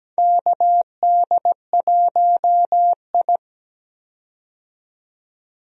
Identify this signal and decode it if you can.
Morse code, heard as a keyed tone, about 17 wpm, 700 Hz. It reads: KD1I
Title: USA Call Signs - 4 Characters - 17wpm